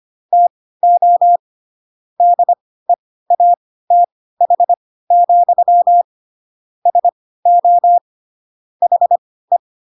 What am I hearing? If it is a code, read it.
Morse code, heard as a keyed tone, about 25 wpm, spaced as for 16 wpm, 700 Hz. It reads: TO DEATH, SO HE
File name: Pinocchio / Pinocchio ch1 10 25 ik1hge